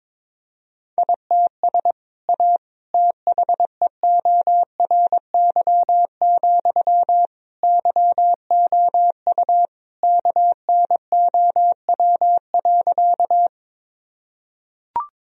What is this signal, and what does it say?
Morse code, heard as a keyed tone, about 22 wpm, 700 Hz. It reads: ITS A THEORY, YOU KNOW.
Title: Sentences from Top 1000 Words - Review Part 3 - 22wpm